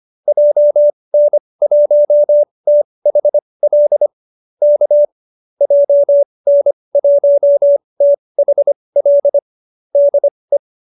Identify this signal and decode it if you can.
Morse code, heard as a keyed tone, about 25 wpm, spaced as for 20 wpm, 590 Hz. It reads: JN1THL K JN1THL DE